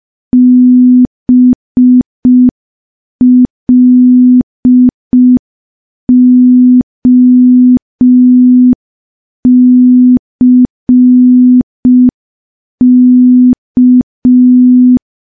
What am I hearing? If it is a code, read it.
Morse code, heard as a keyed tone, about 5 wpm, 250 Hz. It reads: BLOCK
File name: morse / morse-training-05WPM